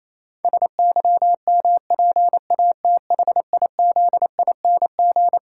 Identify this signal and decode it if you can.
Morse code, heard as a keyed tone, about 28 wpm, 700 Hz. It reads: SYMPATHIZING
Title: Pangram Challenge v2 - 28wpm